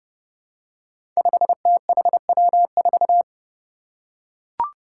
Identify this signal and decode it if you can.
Morse code, heard as a keyed tone, about 31 wpm, 700 Hz. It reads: 5THW4
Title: Five Challenging Letter-Number Combinations 30wpm